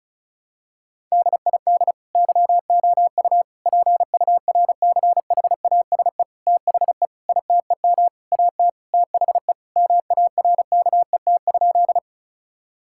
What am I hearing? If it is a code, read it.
Morse code, heard as a keyed tone, about 36 wpm, 700 Hz. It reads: DID YOU PURCHASE THE ITEM AT THE MARKET?
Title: Sentences from Top 1000 Words - Review Part 6 - 35wpm